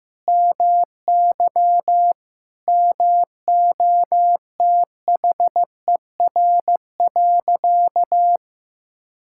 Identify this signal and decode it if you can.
Morse code, heard as a keyed tone, about 15 wpm, 700 Hz. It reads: MY MOTHER.
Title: Sentences from Top 1600 Words 15wpm